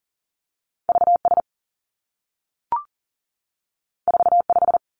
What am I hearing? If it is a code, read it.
Morse code, heard as a keyed tone, about 41 wpm, 700 Hz. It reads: V S 4 5
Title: Two Challenging Letter-Number Combinations 40wpm